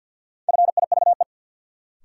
Morse code, heard as a keyed tone, about 53 wpm, 700 Hz. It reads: FIVE
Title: Sentences from Top 400 Words - New Word Drill - 50wpm